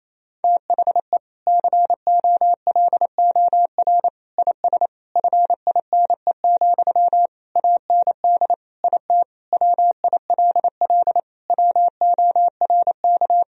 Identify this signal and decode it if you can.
Morse code, heard as a keyed tone, about 28 wpm, 700 Hz. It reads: THE COLOR IS FINE, AND IT WILL WORK
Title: Sentences from Top 1000 Words - Review Part 2 - 28wpm